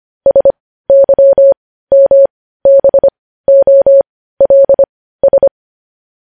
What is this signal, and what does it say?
Morse code, heard as a keyed tone, about 25 wpm, spaced as for 15 wpm, 550 Hz. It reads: SYMBOLS
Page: news headlines in morse code at 15 wpm